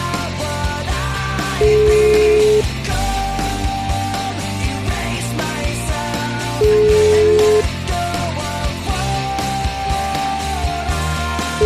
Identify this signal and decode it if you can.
Ringback tone (Europe, 425 Hz), with noise.